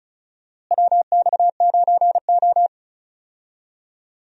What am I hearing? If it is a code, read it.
Morse code, heard as a keyed tone, about 36 wpm, 700 Hz. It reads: WX9O
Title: USA Call Signs - 4 Characters - Repeated 2x - 35wpm